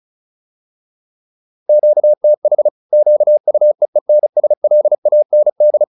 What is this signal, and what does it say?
Morse code, heard as a keyed tone, about 34 wpm, 600 Hz. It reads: QTH QUEENSLAND
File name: QSO407 35